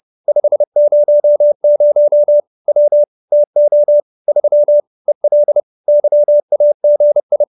Morse code, heard as a keyed tone, about 30 wpm, 590 Hz. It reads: 500 W TO 3 EL YAGI
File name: QSO319 30